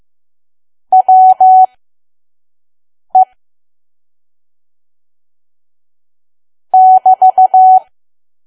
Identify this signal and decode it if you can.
Morse code, heard as a keyed tone, about 15 wpm, spaced as for 5 wpm, 750 Hz. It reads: WE <BT>
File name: Archive / 5 WPM files / PM